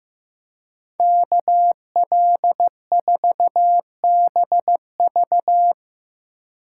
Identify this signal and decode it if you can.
Morse code, heard as a keyed tone, about 17 wpm, 700 Hz. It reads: KL4BV
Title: Speed Racing - USA 2x2 Call Signs 15wpm